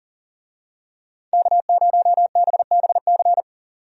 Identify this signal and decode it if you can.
Morse code, heard as a keyed tone, about 37 wpm, 700 Hz. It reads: K0BBC
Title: USA Call Signs - 5 Characters - 40wpm